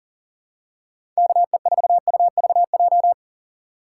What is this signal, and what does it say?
Morse code, heard as a keyed tone, about 40 wpm, 700 Hz. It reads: KE4UVJ
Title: USA Call Signs - 4x Character Spacing - 40wpm